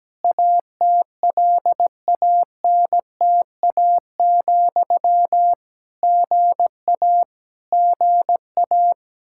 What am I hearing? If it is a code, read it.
Morse code, heard as a keyed tone, about 17 wpm, 700 Hz. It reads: ATLANTA, GA GA
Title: K1USN SST QSO Practice 17wpm